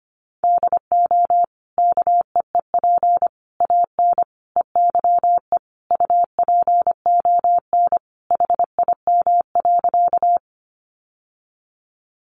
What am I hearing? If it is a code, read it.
Morse code, heard as a keyed tone, about 25 wpm, 700 Hz. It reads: DO KEEP AN EYE UPON HIM.
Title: Sentences from Top 1100 Words 25wpm